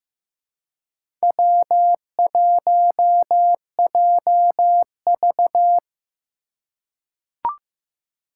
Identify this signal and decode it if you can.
Morse code, heard as a keyed tone, about 15 wpm, 700 Hz. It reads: W1JV
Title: USA Call Signs - 4 Characters - 4x Character Spacing - 15wpm